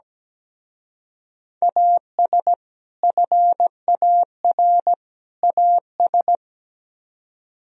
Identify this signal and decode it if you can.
Morse code, heard as a keyed tone, about 20 wpm, 700 Hz. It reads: AS FAR AS
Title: Speed-Racing - Sets of 3 Words 17wpm